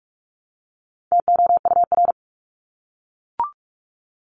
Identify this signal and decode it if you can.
Morse code, heard as a keyed tone, about 44 wpm, 700 Hz. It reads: TOUR
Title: Top 1251-1500 Words 45wpm